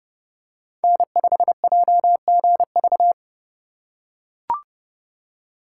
Morse code, heard as a keyed tone, about 30 wpm, 700 Hz. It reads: N5JGV
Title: USA Call Signs - 3x Character Spacing - 30wpm